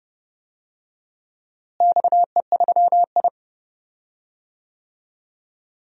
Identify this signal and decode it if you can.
Morse code, heard as a keyed tone, about 30 wpm, 700 Hz. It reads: XE3I
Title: DXCC-100 Call Signs - 4 Characters - Repeated 2x - 30wpm